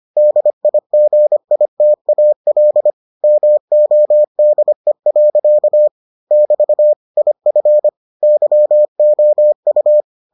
Morse code, heard as a keyed tone, about 25 wpm, 600 Hz. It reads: DIGITAL MODE. <BT> IF YOU